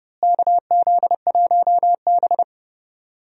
Morse code, heard as a keyed tone, about 31 wpm, 700 Hz. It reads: KZ1B
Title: USA Call Signs - 4 Characters - Repeated 3x - 30wpm